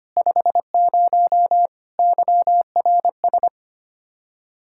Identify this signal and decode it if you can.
Morse code, heard as a keyed tone, about 25 wpm, 700 Hz. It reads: 50 YRS
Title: K1USN SST QSO Practice 25wpm